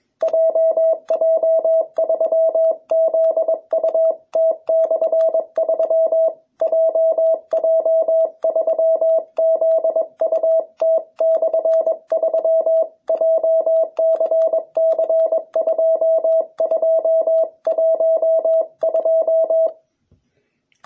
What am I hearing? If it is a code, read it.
Morse code, heard as a keyed tone, about 21 wpm, 650 Hz. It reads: JJ3ZUT/3 JJ3ZUT/3 JCC2212